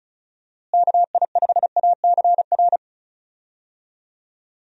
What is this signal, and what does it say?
Morse code, heard as a keyed tone, about 34 wpm, 700 Hz. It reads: KI5ACR
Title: USA Call Signs - 6 Characters - Repeated 2x - 35wpm